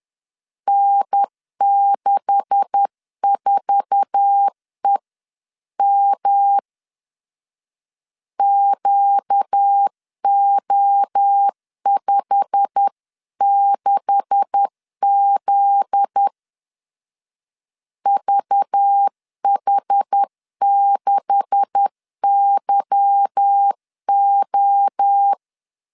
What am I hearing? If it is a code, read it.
Morse code, heard as a keyed tone, about 10 wpm, 800 Hz. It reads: N64EM QO56Z VH6YO
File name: LEZIONI / 9 286 10wpm 2735